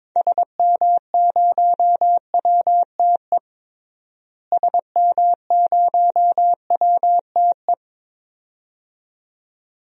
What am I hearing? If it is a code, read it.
Morse code, heard as a keyed tone, about 22 wpm, 700 Hz. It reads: SM0WTE SM0WTE
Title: DXCC-100 Call Signs - 6 Characters - Repeated 2x - 22wpm